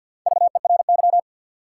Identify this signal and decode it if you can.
Morse code, heard as a keyed tone, about 47 wpm, 700 Hz. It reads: VERY
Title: Sets of 2 Words - New Word Drill - 50wpm